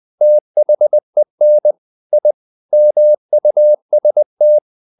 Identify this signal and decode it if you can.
Morse code, heard as a keyed tone, about 20 wpm, 600 Hz. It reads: THEN I MUST
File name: the old man and the sea part 2 20wpm